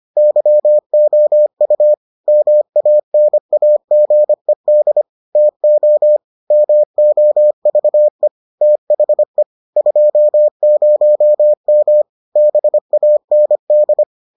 Morse code, heard as a keyed tone, about 25 wpm, 600 Hz. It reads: YOU MANAGED TO MOVE THE 20M BAND